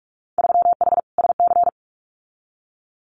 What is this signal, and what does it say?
Morse code, heard as a keyed tone, about 44 wpm, 700 Hz. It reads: 3H SC
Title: Winter Field Day Exchanges 45wpm